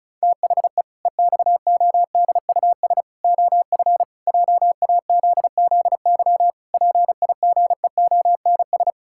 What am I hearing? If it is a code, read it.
Morse code, heard as a keyed tone, about 36 wpm, 700 Hz. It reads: THE EXODUS OF JAZZY PIGEONS
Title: Pangram Challenge v2 - 35wpm